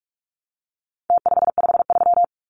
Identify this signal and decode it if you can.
Morse code, heard as a keyed tone, about 44 wpm, 700 Hz. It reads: T553
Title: Numbers - Contest Style - 4-Digit 45wpm